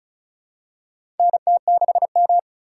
Morse code, heard as a keyed tone, about 36 wpm, 700 Hz. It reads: NT6M
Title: USA Call Signs - 3x Character Spacing - 35wpm